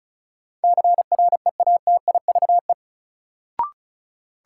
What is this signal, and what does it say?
Morse code, heard as a keyed tone, about 36 wpm, 700 Hz. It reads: CREATIVE
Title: Sentences from Top 1300 Words - New Word Drill Alt - 35wpm